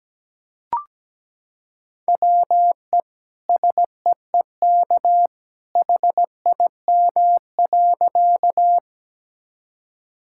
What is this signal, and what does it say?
Morse code, heard as a keyed tone, about 17 wpm, 700 Hz. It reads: WE SEEK HIM.
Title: Sentences from Top 1600 Words 17wpm